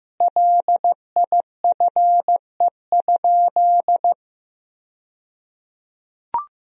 Morse code, heard as a keyed tone, about 16 wpm, 700 Hz. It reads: LIFE?
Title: 2x Word-Spacing - Sentences from Top 300 Words - 15wpm